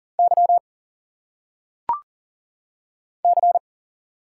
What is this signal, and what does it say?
Morse code, heard as a keyed tone, about 39 wpm, 700 Hz. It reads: Y C
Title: Single Letters - Phonetic Pronunciation 40wpm